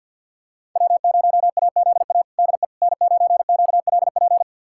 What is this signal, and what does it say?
Morse code, heard as a keyed tone, about 50 wpm, 700 Hz. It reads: W0AZA DE N9QLP
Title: USA Callsign DE Callsign - 5 Characters - 50wpm